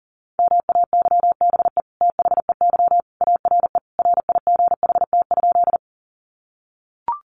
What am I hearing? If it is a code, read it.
Morse code, heard as a keyed tone, about 40 wpm, 700 Hz. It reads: MAYBE THEY ARE RIGHT?
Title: Sentences from Top 1000 Words 40wpm